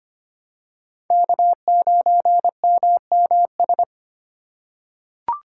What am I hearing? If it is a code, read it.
Morse code, heard as a keyed tone, about 25 wpm, 700 Hz. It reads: K9MMS
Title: K1USN SST QSO Practice - 25wpm